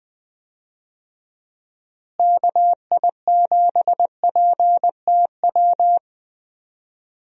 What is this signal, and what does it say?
Morse code, heard as a keyed tone, about 20 wpm, 700 Hz. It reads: KI7PTW
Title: USA Call Signs - 6 Characters - Repeated 2x - 20wpm